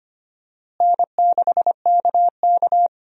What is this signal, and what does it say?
Morse code, heard as a keyed tone, about 25 wpm, 700 Hz. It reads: N6KK
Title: USA Call Signs - 4 Characters - Repeated 2x - 25wpm